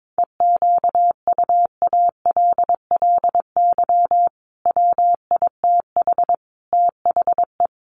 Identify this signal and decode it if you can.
Morse code, heard as a keyed tone, about 22 wpm, 700 Hz. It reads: EQUALLY WITH THE
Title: Sentences from Top 1400 Words 22wpm